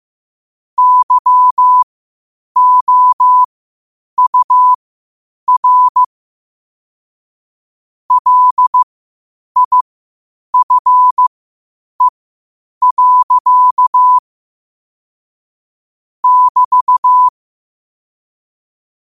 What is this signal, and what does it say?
Morse code, heard as a keyed tone, about 15 wpm, spaced as for 8 wpm, 1000 Hz. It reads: YOUR LIFE. <BT>
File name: qotdfiles / 8wpm